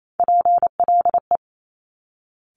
Morse code, heard as a keyed tone, about 28 wpm, 700 Hz. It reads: PLE
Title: Top Three Letter Combinations in Words 28wpm